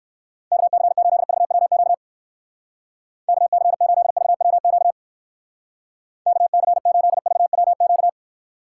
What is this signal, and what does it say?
Morse code, heard as a keyed tone, about 57 wpm, 700 Hz. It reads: KC8VWQ KC8VWQ KC8VWQ